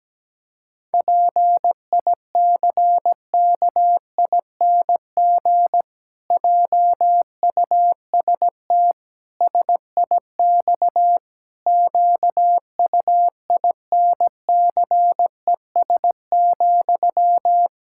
Morse code, heard as a keyed tone, about 17 wpm, 700 Hz. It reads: PICKING JUST SIX QUINCES,
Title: Pangram Challenge v2 - 17wpm